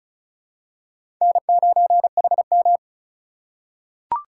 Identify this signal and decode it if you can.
Morse code, heard as a keyed tone, about 36 wpm, 700 Hz. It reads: N9HM
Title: USA Call Signs - 4 Characters - 2x Character Spacing - 35wpm